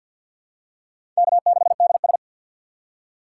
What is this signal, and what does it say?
Morse code, heard as a keyed tone, about 50 wpm, 700 Hz. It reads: K 6 D S
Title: USA Call Signs - 6x Character Spacing - 50wpm